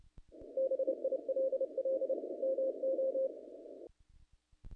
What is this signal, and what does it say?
Morse code, heard as a keyed tone, about 29 wpm, 530 Hz. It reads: BILL MO
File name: contests / naqp-cw-k5tr-op-k5pi- -1900